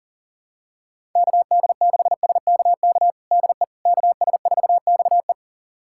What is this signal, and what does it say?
Morse code, heard as a keyed tone, about 37 wpm, 700 Hz. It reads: KD6SKK DE KS4XE
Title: USA Callsign DE Callsign 40wpm